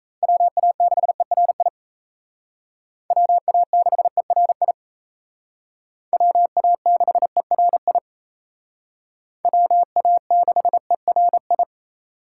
Morse code, heard as a keyed tone, about 35 wpm, 700 Hz. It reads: WA6ERI WA6ERI WA6ERI WA6ERI